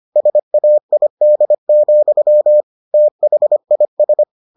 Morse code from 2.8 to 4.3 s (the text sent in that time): THIS